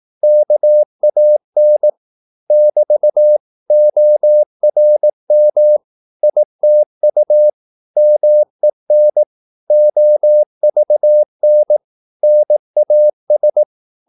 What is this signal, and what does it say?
Morse code, heard as a keyed tone, about 18 wpm, 600 Hz. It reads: KAN <BT> ORM ITU MEN OVN NAS
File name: Korte ord DK 18wpm